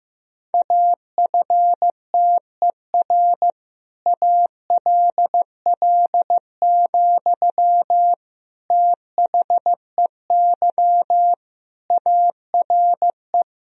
Morse code, heard as a keyed tone, about 15 wpm, 700 Hz. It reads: AFTER ALL, THEY ARE
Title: Sentences from Top 1000 Words - Review Part 1 - 15wpm